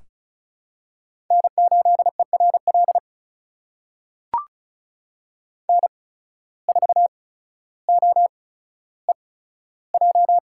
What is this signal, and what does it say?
Morse code, heard as a keyed tone, about 36 wpm, spaced as for 15 wpm, 700 Hz. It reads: N8ERL N4OEJ
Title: USA Call Signs - 5 Characters - 8x Character Spacing - 35wpm